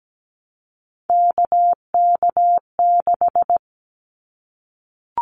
Morse code, heard as a keyed tone, about 17 wpm, 700 Hz. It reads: KK6
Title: Callsign Prefixes for Top 25 DXCC - Repeated 3x - 17wpm